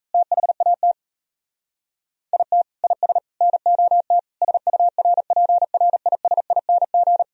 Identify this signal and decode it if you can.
Morse code, heard as a keyed tone, about 32 wpm, 700 Hz. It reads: THAT IT IS NOT SURPRISING
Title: Speed-Racing - Sets of 5 Words 28wpm